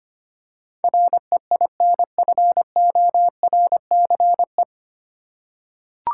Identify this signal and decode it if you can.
Morse code, heard as a keyed tone, about 25 wpm, 700 Hz. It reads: REINFORCE
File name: Nine Letter Words - 25wpm